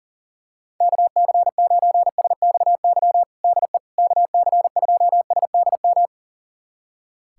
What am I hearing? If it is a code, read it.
Morse code, heard as a keyed tone, about 40 wpm, 700 Hz. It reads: KC9SXY DE KC2SDM
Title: USA Callsign DE Callsign - 6 Characters - 40wpm